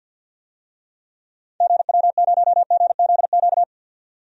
Morse code, heard as a keyed tone, about 52 wpm, 700 Hz. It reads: GW0GZQ
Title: DXCC-100 Call Signs - 6 Characters - 50wpm